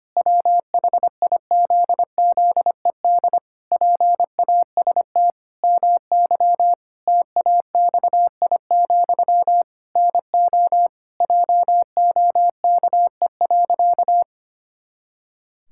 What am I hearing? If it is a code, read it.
Morse code, heard as a keyed tone, about 25 wpm, 700 Hz. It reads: WHIZZED PAST MY TAXI, NO JOKE.
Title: Pangram Challenge v2 - 25wpm